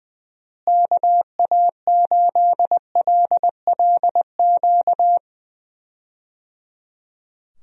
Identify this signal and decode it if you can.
Morse code, heard as a keyed tone, about 20 wpm, 700 Hz. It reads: KA8LLQ
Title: USA Call Signs - 6 Characters - Repeated 3x - 20wpm